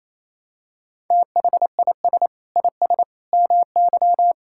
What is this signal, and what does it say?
Morse code, heard as a keyed tone, about 27 wpm, 700 Hz. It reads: THIS IS MY
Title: Sentences from Top 1600 Words 28wpm